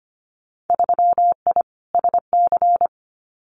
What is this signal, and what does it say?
Morse code, heard as a keyed tone, about 25 wpm, 700 Hz. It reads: 3I SC
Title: Winter Field Day Exchanges Encore 25wpm